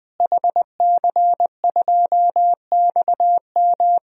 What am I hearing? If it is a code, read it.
Morse code, heard as a keyed tone, about 21 wpm, 700 Hz. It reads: HC2XM
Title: DXCC-100 Call Signs - 5 Characters - Repeated 2x - 20wpm